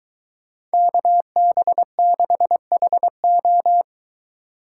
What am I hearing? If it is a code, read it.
Morse code, heard as a keyed tone, about 23 wpm, 700 Hz. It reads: KB6HO